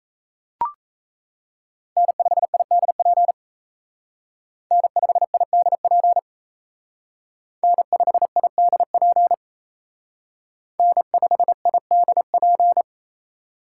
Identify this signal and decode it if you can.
Morse code, heard as a keyed tone, about 32 wpm, 700 Hz. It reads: N5IDP N5IDP N5IDP N5IDP
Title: Speed Racing - USA 1x3 Call Signs 28wpm